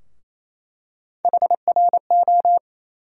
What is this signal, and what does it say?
Morse code, heard as a keyed tone, about 28 wpm, 700 Hz. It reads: HRO
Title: USA Call Sign Postfixes - 3 Letters 28wpm